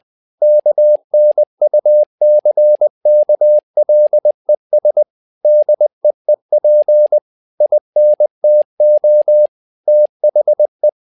Morse code, heard as a keyed tone, about 20 wpm, 600 Hz. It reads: KNUCKLES DEEP INTO THE
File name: learning center / tower trouble / tower trouble 20wpm